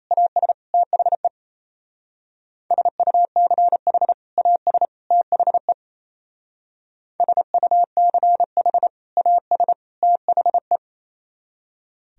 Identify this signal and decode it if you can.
Morse code, heard as a keyed tone, about 37 wpm, 700 Hz. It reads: AS THE SUCH AS THE SUCH AS THE